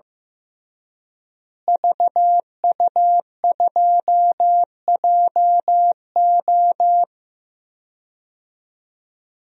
Morse code, heard as a keyed tone, about 15 wpm, 700 Hz. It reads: VU2JO
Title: DXCC-100 Call Signs - 5 Characters - Repeated 3x - 15wpm